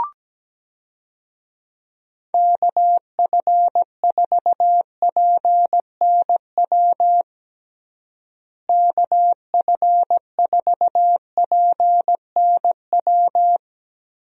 Morse code, heard as a keyed tone, about 17 wpm, 700 Hz. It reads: KF4PNW KF4PNW
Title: USA Call Signs - 6 Characters - Repeated 2x - 17wpm